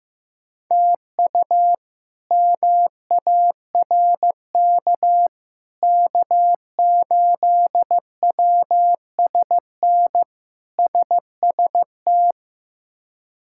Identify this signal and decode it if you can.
Morse code, heard as a keyed tone, about 15 wpm, 700 Hz. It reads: TU MARK K8WSN SST
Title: K1USN SST QSO Practice - 15wpm